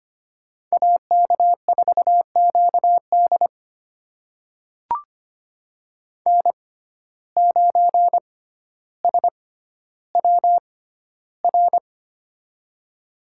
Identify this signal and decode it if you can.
Morse code, heard as a keyed tone, about 25 wpm, spaced as for 13 wpm, 700 Hz. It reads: AK4QD N9SWR
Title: USA Call Signs - 5 Characters - 6x Character Spacing - 25wpm